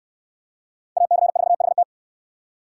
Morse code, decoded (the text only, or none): AC4HT